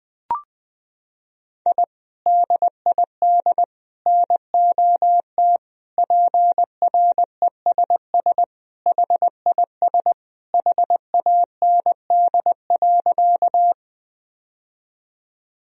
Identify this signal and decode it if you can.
Morse code, heard as a keyed tone, about 20 wpm, 700 Hz. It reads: I DID NOT PRESS HIS HAND.
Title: Sentences from Top 1000 Words - Review Part 1 - 20wpm